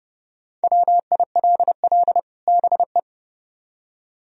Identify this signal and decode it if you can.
Morse code, heard as a keyed tone, about 29 wpm, 700 Hz. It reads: WILL BE